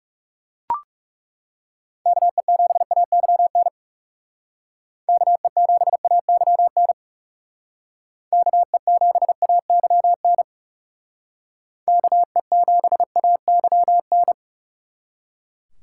Morse code, heard as a keyed tone, about 38 wpm, 700 Hz. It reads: KE7AYN KE7AYN KE7AYN KE7AYN